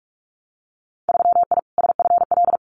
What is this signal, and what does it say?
Morse code, heard as a keyed tone, about 46 wpm, 700 Hz. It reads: 3I SFL
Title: Winter Field Day Exchanges 45wpm